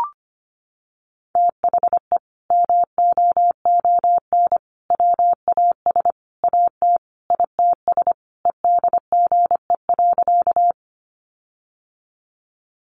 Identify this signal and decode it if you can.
Morse code, heard as a keyed tone, about 25 wpm, 700 Hz. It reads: THE MOON WAS AT ITS EDGE.
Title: Sentences from Top 1300 Words 25wpm